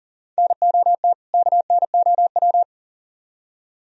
Morse code, heard as a keyed tone, about 52 wpm, 700 Hz. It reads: NOT KNOW